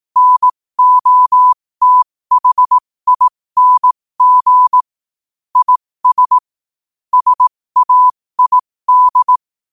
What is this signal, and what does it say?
Morse code, heard as a keyed tone, about 18 wpm, spaced as for 15 wpm, 1000 Hz. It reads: NOTHING IS SAID